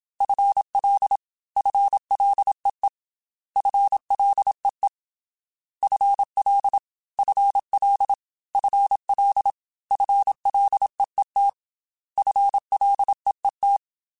Morse code, heard as a keyed tone, about 27 wpm, 800 Hz. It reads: FL FLEE FLEE FL FL FL FLEET FLEET
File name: FL / FL mp3s / FL Prefixes 3x2 Twice @25